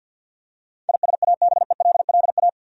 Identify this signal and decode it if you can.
Morse code, heard as a keyed tone, about 52 wpm, 700 Hz. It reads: ISABELLA
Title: USA Common Names - 50wpm